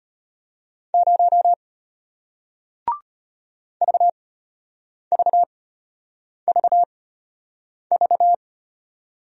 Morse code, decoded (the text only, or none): 0 VVVV